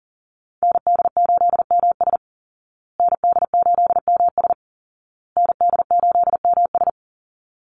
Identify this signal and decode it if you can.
Morse code, heard as a keyed tone, about 40 wpm, 700 Hz. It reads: ND8MS ND8MS ND8MS
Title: Call Signs Repeated 2x - 40wpm